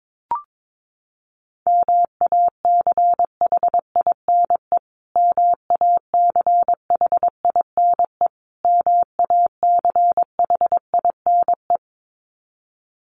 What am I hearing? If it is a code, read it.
Morse code, heard as a keyed tone, about 22 wpm, 700 Hz. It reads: MACHINE MACHINE MACHINE
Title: Top 500 Words Repeated 22wpm